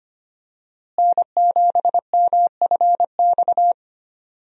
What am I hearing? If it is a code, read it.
Morse code, heard as a keyed tone, about 25 wpm, 700 Hz. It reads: N7MFX
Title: USA Call Signs - 5 Characters - Repeated 2x - 25wpm